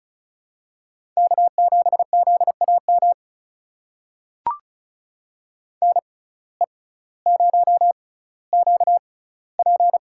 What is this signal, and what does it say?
Morse code, heard as a keyed tone, about 36 wpm, spaced as for 18 wpm, 700 Hz. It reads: K7ZAM NE0QP
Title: USA Call Signs - 5 Characters - 6x Character Spacing - 35wpm